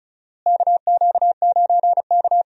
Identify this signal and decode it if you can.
Morse code, heard as a keyed tone, about 36 wpm, 700 Hz. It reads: KQ9K
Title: USA Call Signs - 4 Characters - Repeated 2x - 35wpm